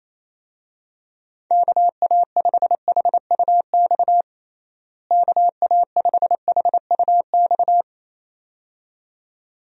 Morse code, heard as a keyed tone, about 28 wpm, 700 Hz. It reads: KA5HUX KA5HUX
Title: USA Call Signs - 6 Characters - Repeated 2x - 28wpm